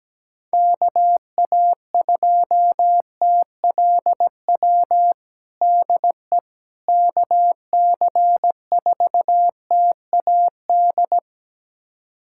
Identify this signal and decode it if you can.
Morse code, heard as a keyed tone, about 17 wpm, 700 Hz. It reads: KA2TLW DE KC4TAD
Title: USA Callsign DE Callsign - 6 Characters - 17wpm